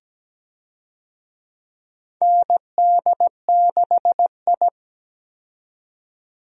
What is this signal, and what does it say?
Morse code, heard as a keyed tone, about 17 wpm, 700 Hz. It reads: ND6I
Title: USA Call Signs - 4 Characters - 17wpm